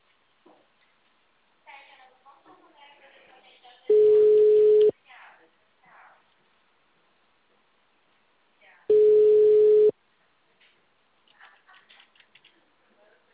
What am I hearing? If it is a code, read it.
Ringback tone (Europe, 425 Hz), with speech.
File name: out- -2004- - - .4503